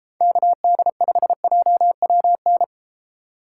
Morse code, heard as a keyed tone, about 32 wpm, 700 Hz. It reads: KD5JWN